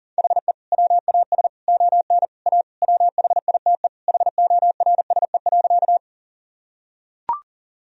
Morse code, heard as a keyed tone, about 40 wpm, 700 Hz. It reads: HE WAS ON A WHITE HORSE.
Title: Sentences from Top 1000 Words - Review Part 1 - 40wpm